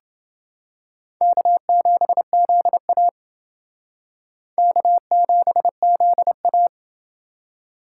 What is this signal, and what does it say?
Morse code, heard as a keyed tone, about 29 wpm, 700 Hz. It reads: K7ZA K7ZA